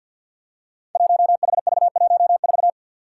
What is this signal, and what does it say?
Morse code, heard as a keyed tone, about 53 wpm, 700 Hz. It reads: 1HV14
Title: Five Challenging Letter-Number Combinations 50wpm